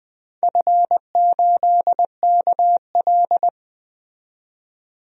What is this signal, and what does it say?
Morse code, heard as a keyed tone, about 20 wpm, 700 Hz. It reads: F8KL
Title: DXCC-100 Call Signs - 4 Characters - 20wpm